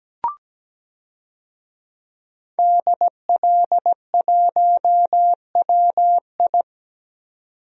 Morse code, heard as a keyed tone, about 17 wpm, 700 Hz. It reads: DL1WI
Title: DXCC-100 Call Signs - 5 Characters - 17wpm